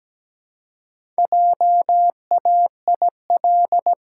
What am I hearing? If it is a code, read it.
Morse code, heard as a keyed tone, about 17 wpm, 700 Hz. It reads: JAIL
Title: Pangram Challenge v2 - 17wpm